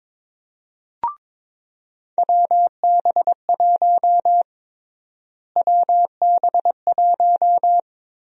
Morse code, heard as a keyed tone, about 21 wpm, 700 Hz. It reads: WB1 WB1
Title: USA Call Sign Prefixes - Repeated 2x - 22wpm